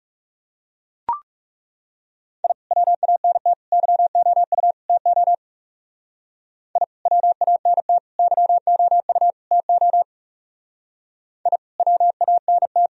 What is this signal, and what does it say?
Morse code, heard as a keyed tone, about 38 wpm, 700 Hz. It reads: I WANT YOU TO I WANT YOU TO I WANT